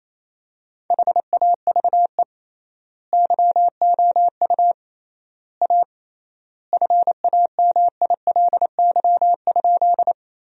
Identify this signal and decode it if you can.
Morse code, heard as a keyed tone, about 28 wpm, 700 Hz. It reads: HAVE YOU A FAMILY?
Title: 3x Word-Spacing - Sentences from Top 400 Words - 28wpm